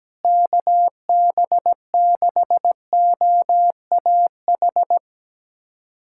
Morse code, heard as a keyed tone, about 17 wpm, 700 Hz. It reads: KB6OAH
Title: USA Callsign DE Callsign 17wpm